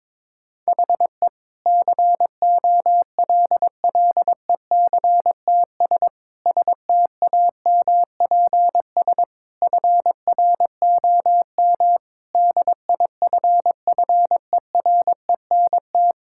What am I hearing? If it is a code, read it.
Morse code, heard as a keyed tone, about 22 wpm, 700 Hz. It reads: HE COLLECTS STAMPS FROM DIFFERENT
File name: Difficult Sentences with Hints - 22wpm